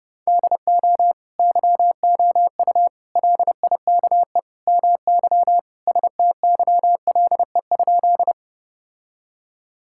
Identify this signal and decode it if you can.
Morse code, heard as a keyed tone, about 30 wpm, 700 Hz. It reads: DO YOU LIKE MY STYLE?
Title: Sentences from Top 1000 Words - Review Part 5 - 30wpm